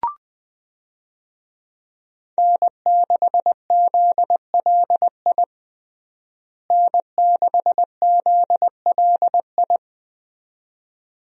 Morse code, heard as a keyed tone, about 20 wpm, 700 Hz. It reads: N6ZLI N6ZLI
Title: USA Call Signs - 5 Characters - Repeated 2x - 20wpm